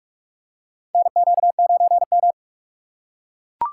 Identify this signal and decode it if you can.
Morse code, heard as a keyed tone, about 44 wpm, 700 Hz. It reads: NQ9M
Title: USA Call Signs - 6x Character Spacing - 45wpm